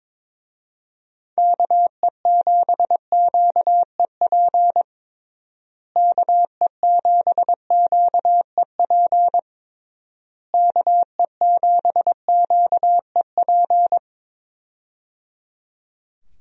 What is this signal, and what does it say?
Morse code, heard as a keyed tone, about 22 wpm, 700 Hz. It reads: KE7QEP KE7QEP KE7QEP
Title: USA Call Signs - 6 Characters - Repeated 3x - 22wpm